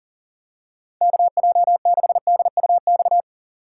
Morse code, heard as a keyed tone, about 40 wpm, 700 Hz. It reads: KJ6DUX
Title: USA Call Signs - 6 Characters - 2x Character Spacing - 40wpm